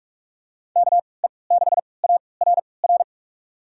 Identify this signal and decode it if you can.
Morse code, heard as a keyed tone, about 43 wpm, spaced as for 25 wpm, 700 Hz. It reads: KE6ARR
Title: USA Call Signs - 6 Characters - 3x Character Spacing - 45wpm